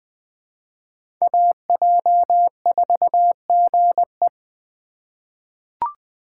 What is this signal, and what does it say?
Morse code, heard as a keyed tone, about 20 wpm, 700 Hz. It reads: AJ4GE
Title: USA Call Signs - 5 Characters - 8x Character Spacing - 20wpm